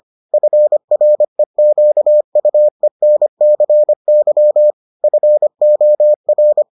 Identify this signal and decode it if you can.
Morse code, heard as a keyed tone, about 25 wpm, 600 Hz. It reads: FREQUENCY FOR